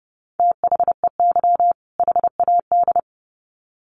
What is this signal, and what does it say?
Morse code, heard as a keyed tone, about 30 wpm, 700 Hz. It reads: THEY HAD
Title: Sets of 2 Words 30wpm